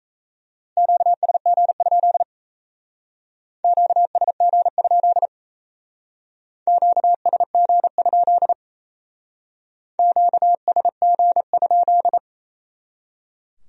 Morse code, heard as a keyed tone, about 36 wpm, 700 Hz. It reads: QSG? QSG? QSG? QSG?